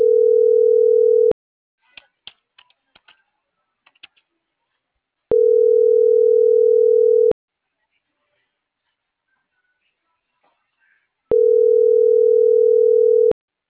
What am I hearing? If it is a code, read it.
Ringback tone (North America), 440 + 480 Hz.